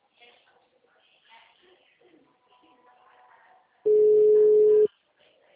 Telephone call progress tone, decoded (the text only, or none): ringback tone